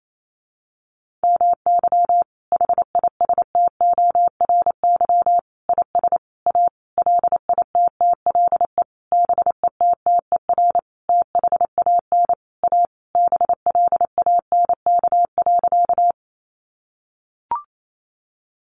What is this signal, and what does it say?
Morse code, heard as a keyed tone, about 28 wpm, 700 Hz. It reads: MY HISTORY IS A LITTLE BETTER THAN A BLANK.
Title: Sentences from Top 1500 Words 28wpm